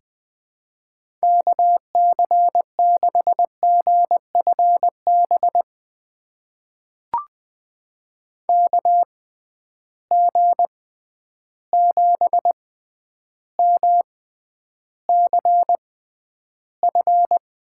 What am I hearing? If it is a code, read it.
Morse code, heard as a keyed tone, about 20 wpm, spaced as for 10 wpm, 700 Hz. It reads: KC6GFB KG7MCF
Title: USA Call Signs - 6 Characters - 6x Character Spacing - 20wpm